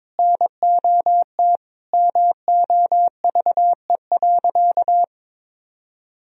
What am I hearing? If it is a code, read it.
Morse code, heard as a keyed tone, about 21 wpm, 700 Hz. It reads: NOT MOVE.
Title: Sentences from Top 1000 Words - Review Part 1 - 22wpm